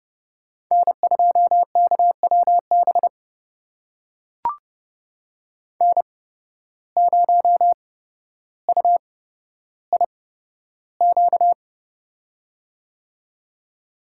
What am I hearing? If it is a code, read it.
Morse code, heard as a keyed tone, about 30 wpm, spaced as for 13 wpm, 700 Hz. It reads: N2KWB N0UIQ
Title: USA Call Signs - 5 Characters - 8x Character Spacing - 30wpm